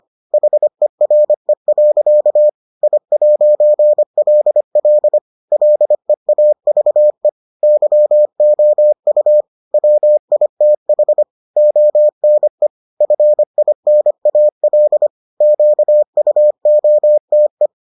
Morse code, heard as a keyed tone, about 25 wpm, 600 Hz. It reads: HERE. I'LL LEAVE YOU WITH ONE FINAL QUOTE